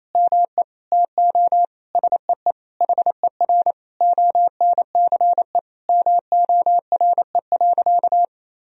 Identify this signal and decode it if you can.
Morse code, heard as a keyed tone, about 28 wpm, 700 Hz. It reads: ME TO SEE HER ONCE MORE.
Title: Sentences from Top 500 Words and Abbreviations 28wpm